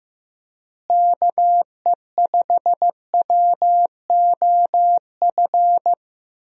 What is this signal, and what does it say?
Morse code, heard as a keyed tone, about 15 wpm, 700 Hz. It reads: KE5WOF
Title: USA Call Signs - 6 Characters - Repeated 3x - 15wpm